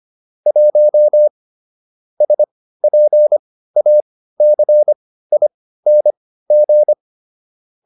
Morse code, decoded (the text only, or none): 1 SPACING